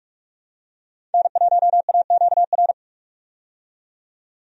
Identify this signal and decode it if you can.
Morse code, heard as a keyed tone, about 47 wpm, 700 Hz. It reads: N1AQR
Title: USA Call Signs - 5 Characters - 45wpm